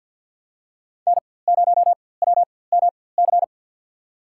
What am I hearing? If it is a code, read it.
Morse code, heard as a keyed tone, about 51 wpm, spaced as for 23 wpm, 700 Hz. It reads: N0WMC
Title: USA Call Signs - 4x Character Spacing - 50wpm